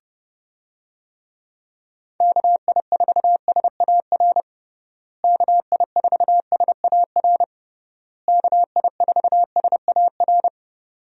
Morse code, heard as a keyed tone, about 30 wpm, 700 Hz. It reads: KI4SAR KI4SAR KI4SAR
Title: USA Call Signs - 6 Characters - Repeated 3x - 30wpm